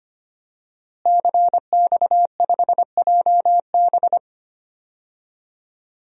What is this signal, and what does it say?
Morse code, heard as a keyed tone, about 25 wpm, 700 Hz. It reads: CX5JB
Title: DXCC-100 Call Signs - 5 Characters - Repeated 3x - 25wpm